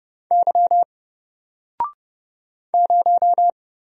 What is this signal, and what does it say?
Morse code, heard as a keyed tone, about 30 wpm, 700 Hz. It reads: Y 0
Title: Single Letter-Number Alt 30wpm